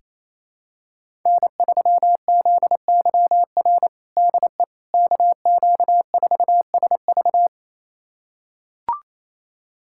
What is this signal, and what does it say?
Morse code, heard as a keyed tone, about 27 wpm, 700 Hz. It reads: N3ZYR DE KQ4SV
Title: USA Callsign DE Callsign - 5 Characters - 28wpm